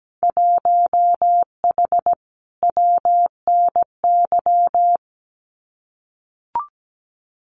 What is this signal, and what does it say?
Morse code, heard as a keyed tone, about 17 wpm, 700 Hz. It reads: 1H WNY
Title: Winter Field Day Exchanges 17wpm